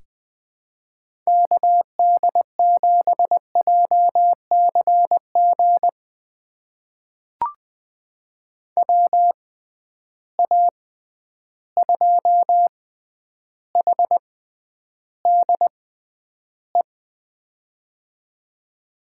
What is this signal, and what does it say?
Morse code, heard as a keyed tone, about 20 wpm, spaced as for 10 wpm, 700 Hz. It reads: KD7JCG WA2HDE
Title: USA Call Signs - 6 Characters - 6x Character Spacing - 20wpm